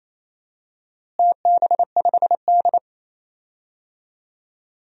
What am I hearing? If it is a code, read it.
Morse code, heard as a keyed tone, about 27 wpm, 700 Hz. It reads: TB5D
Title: DXCC-100 Call Signs - 4 Characters - Repeated 3x - 28wpm